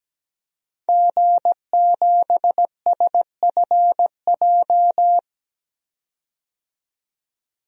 Morse code, heard as a keyed tone, about 17 wpm, 700 Hz. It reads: G7SFJ
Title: DXCC-100 Call Signs - 5 Characters - 17wpm